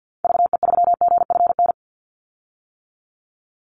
Morse code, heard as a keyed tone, about 50 wpm, 700 Hz. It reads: VE3GFN
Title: Call Signs - 50wpm